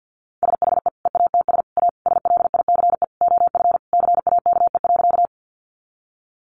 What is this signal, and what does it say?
Morse code, heard as a keyed tone, about 50 wpm, 700 Hz. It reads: SHE EATS A SLICE OF CAKE.
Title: Easy Phrases 50wpm